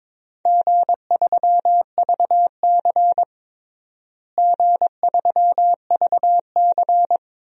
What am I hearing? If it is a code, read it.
Morse code, heard as a keyed tone, about 22 wpm, 700 Hz. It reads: G3VC G3VC
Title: DXCC-100 Call Signs - 4 Characters - Repeated 2x - 22wpm